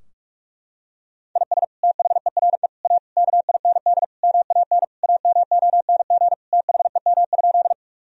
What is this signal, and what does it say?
Morse code, heard as a keyed tone, about 46 wpm, 700 Hz. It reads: IS THERE A KIND MAN AMONG THEM?
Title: Sentences from Top 1000 Words - Review Part 3 - 45wpm